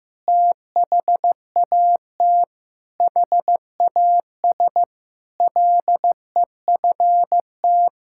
Morse code, heard as a keyed tone, about 15 wpm, 700 Hz. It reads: THAT HAS LEFT
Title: Sentences from Top 1600 Words 15wpm